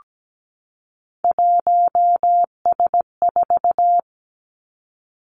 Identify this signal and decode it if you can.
Morse code, heard as a keyed tone, about 17 wpm, 700 Hz. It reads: 1S4
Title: Three Challenging Letter-Number Combinations 17wpm